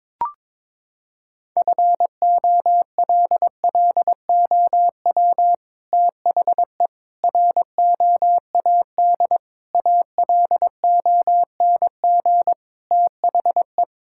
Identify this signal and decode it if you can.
Morse code, heard as a keyed tone, about 22 wpm, 700 Hz. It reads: FOLLOW THE ROAD ALONG THE
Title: Sentences from Top 1000 Words - Review Part 6 - 22wpm